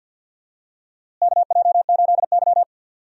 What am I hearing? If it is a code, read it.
Morse code, heard as a keyed tone, about 49 wpm, 700 Hz. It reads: KJ8Y
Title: USA Call Signs - 4 Characters - Repeated 2x - 50wpm